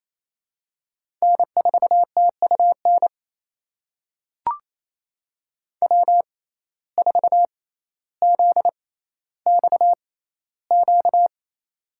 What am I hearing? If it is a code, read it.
Morse code, heard as a keyed tone, about 27 wpm, spaced as for 14 wpm, 700 Hz. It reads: N4TUN W4ZXQ
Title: USA Call Signs - 6x Character Spacing - 28wpm